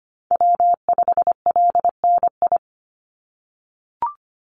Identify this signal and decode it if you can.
Morse code, heard as a keyed tone, about 25 wpm, 700 Hz. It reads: W5LNI
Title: Call Signs - 25wpm